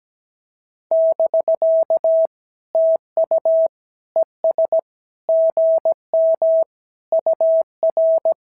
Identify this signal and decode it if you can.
Morse code, heard as a keyed tone, about 17 wpm, 650 Hz. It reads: <BK> TU ES GM UR
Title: SOTA QSO Practice 17wpm